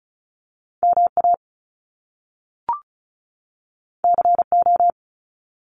Morse code, heard as a keyed tone, about 36 wpm, 700 Hz. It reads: MA CO
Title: US States 35wpm